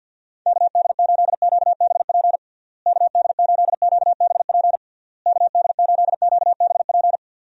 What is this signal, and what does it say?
Morse code, heard as a keyed tone, about 50 wpm, 700 Hz. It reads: KD8QBP KD8QBP KD8QBP
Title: USA Call Signs - 6 Characters - Repeated 3x - 50wpm